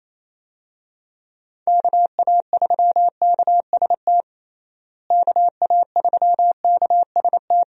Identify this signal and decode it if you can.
Morse code, heard as a keyed tone, about 28 wpm, 700 Hz. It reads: KA3KST KA3KST
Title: USA Call Signs - 6 Characters - Repeated 3x - 28wpm